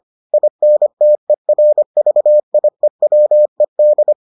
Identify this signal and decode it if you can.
Morse code, heard as a keyed tone, about 25 wpm, 600 Hz. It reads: INTERVIEWED